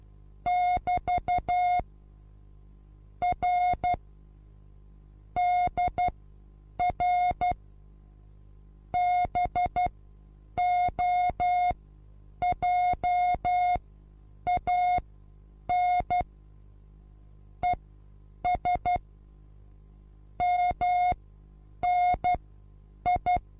Morse code, heard as a keyed tone, about 12 wpm, spaced as for 8 wpm, 720 Hz. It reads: <BT> R DR BOJAN ES MNI
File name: Onlinelehrgang / morsen / DJ4UF Morsen23